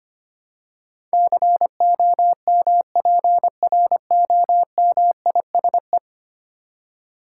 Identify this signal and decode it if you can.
Morse code, heard as a keyed tone, about 25 wpm, 700 Hz. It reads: COMPROMISE
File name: 10 Letter Words - First 3 Letters Given - 25wpm